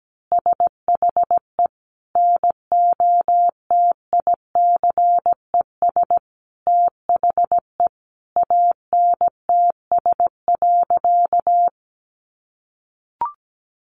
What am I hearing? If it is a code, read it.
Morse code, heard as a keyed tone, about 17 wpm, 700 Hz. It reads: SHE NOTICES THE ANTS.
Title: Easy Phrases 17wpm